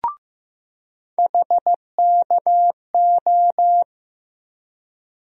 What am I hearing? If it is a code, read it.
Morse code, heard as a keyed tone, about 15 wpm, 700 Hz. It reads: HKO
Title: USA Call Sign Postfixes - 3 Letters 15wpm